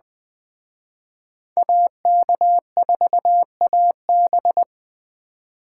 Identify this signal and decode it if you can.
Morse code, heard as a keyed tone, about 20 wpm, 700 Hz. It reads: AK4AB
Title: Speed Racing - USA Call Signs 17wpm